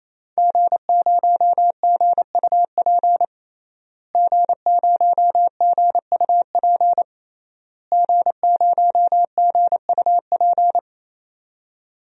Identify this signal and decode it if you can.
Morse code, heard as a keyed tone, about 27 wpm, 700 Hz. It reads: G0GUP G0GUP G0GUP
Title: DXCC-100 Call Signs - 5 Characters - Repeated 3x - 28wpm